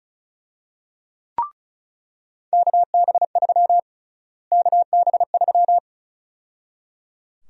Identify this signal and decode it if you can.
Morse code, heard as a keyed tone, about 34 wpm, 700 Hz. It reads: KB3 KB3
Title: USA Call Sign Prefixes - Repeated 2x - 35wpm